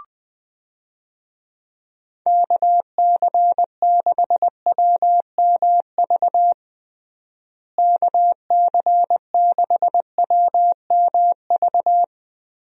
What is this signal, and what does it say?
Morse code, heard as a keyed tone, about 19 wpm, 700 Hz. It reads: KC6WMV KC6WMV
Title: USA Call Signs - 6 Characters - Repeated 3x - 20wpm